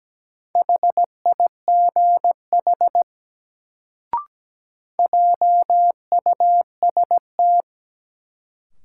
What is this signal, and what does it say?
Morse code, heard as a keyed tone, about 17 wpm, 700 Hz. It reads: HIGH JUST
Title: Sentences from Top 200 Words - New Word Drill - 17wpm